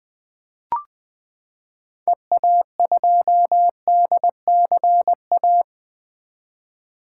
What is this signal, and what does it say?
Morse code, heard as a keyed tone, about 20 wpm, 700 Hz. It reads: EA2DCA
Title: 2023 SOTA Activator Call Signs - 20wpm